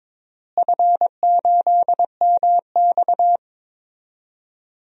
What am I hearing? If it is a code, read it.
Morse code, heard as a keyed tone, about 22 wpm, 700 Hz. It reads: F8MX
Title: DXCC-100 Call Signs - 4 Characters - Repeated 3x - 22wpm